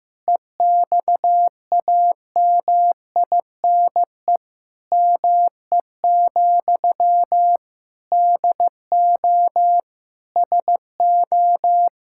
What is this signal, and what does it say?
Morse code, heard as a keyed tone, about 15 wpm, 700 Hz. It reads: EXAMINE ME, DO SO
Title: Sentences from Top 1600 Words 15wpm